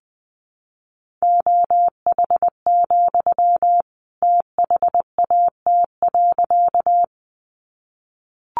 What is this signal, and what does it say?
Morse code, heard as a keyed tone, about 20 wpm, 700 Hz. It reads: OH, THAT.
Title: Sentences from Top 500 Words 20wpm